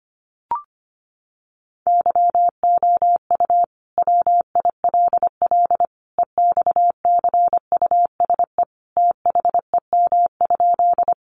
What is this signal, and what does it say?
Morse code, heard as a keyed tone, about 25 wpm, 700 Hz. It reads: YOU WILL EXCUSE THEM?
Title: Sentences from Top 700 Words 25wpm